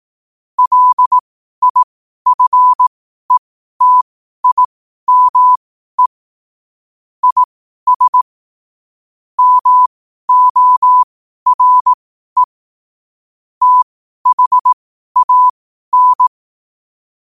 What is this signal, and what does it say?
Morse code, heard as a keyed tone, about 18 wpm, spaced as for 12 wpm, 1000 Hz. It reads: LIFETIME IS MORE THAN
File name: qotdfiles / 12wpm